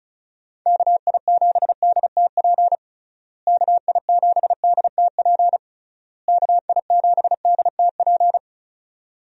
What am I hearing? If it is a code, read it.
Morse code, heard as a keyed tone, about 34 wpm, 700 Hz. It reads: KI7DTP KI7DTP KI7DTP
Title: USA Call Signs - 6 Characters - Repeated 3x - 35wpm